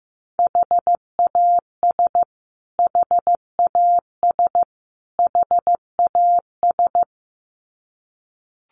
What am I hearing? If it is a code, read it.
Morse code, heard as a keyed tone, about 14 wpm, 700 Hz. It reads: HAS HAS HAS
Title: Top 200 Words Repeated 15wpm